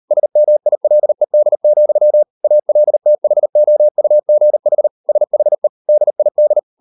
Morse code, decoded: SMILED, ALTHOUGH SHE DID

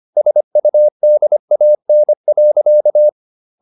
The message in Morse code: SUDAN.

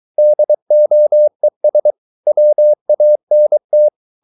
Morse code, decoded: DOES WANT

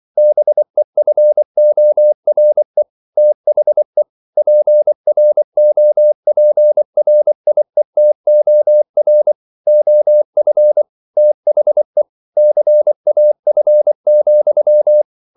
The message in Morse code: BEFORE THE PROPRIETOR OF THE CAF,

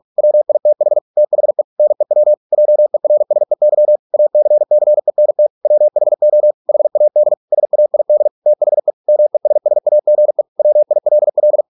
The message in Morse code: WITH THE NEW JERSEY ACCENT WHO HAD SAID THE MESSAGE WILL

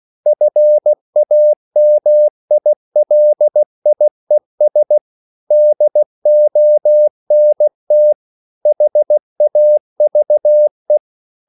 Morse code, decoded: FAMILIES DONT HAVE

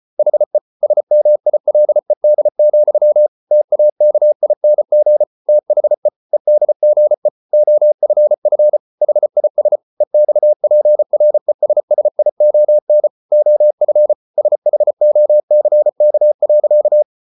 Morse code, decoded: HE SMILED, TAKING THE EDGE OFF HIS EXPRESSION OF SHOCK.

